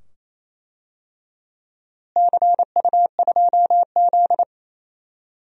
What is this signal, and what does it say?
Morse code, heard as a keyed tone, about 28 wpm, 700 Hz. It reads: CU2Z
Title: DXCC-100 Call Signs - 4 Characters - Repeated 2x - 28wpm